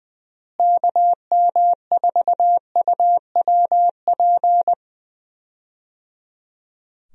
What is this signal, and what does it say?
Morse code, heard as a keyed tone, about 20 wpm, 700 Hz. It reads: KM4UWP
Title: USA Call Signs - 6 Characters - Repeated 3x - 20wpm